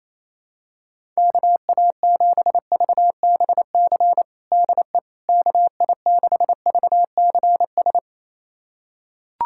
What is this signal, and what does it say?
Morse code, heard as a keyed tone, about 27 wpm, 700 Hz. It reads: KA7VBC DE KI6VCS
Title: USA Callsign DE Callsign - 6 Characters - 28wpm